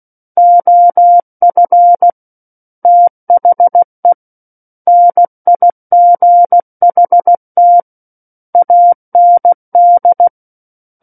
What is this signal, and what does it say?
Morse code, heard as a keyed tone, about 16 wpm, 700 Hz. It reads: OF THE NIGHT AND
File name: Book One / 07-Chapter 7 - 16 WPM